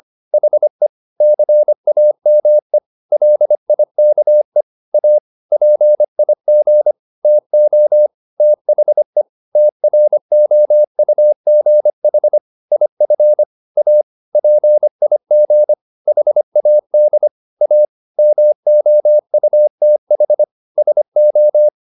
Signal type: Morse code